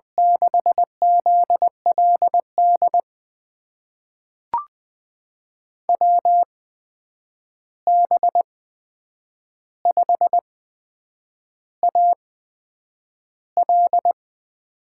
Morse code, heard as a keyed tone, about 20 wpm, spaced as for 8 wpm, 700 Hz. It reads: 6ZLD WB5AL
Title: USA Call Signs - 6 Characters - 8x Character Spacing - 20wpm